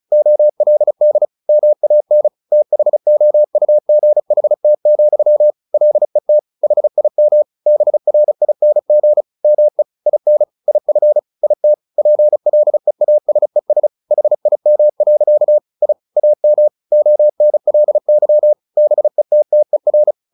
Morse code, heard as a keyed tone, about 34 wpm, 600 Hz. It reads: OLD MAN THOUGHT, LET HIM BRING ME IN IF IT PLEASES HIM. I AM ONLY BETTER
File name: the old man and the sea 2 35wpm